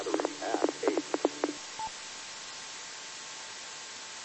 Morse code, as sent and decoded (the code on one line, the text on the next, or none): none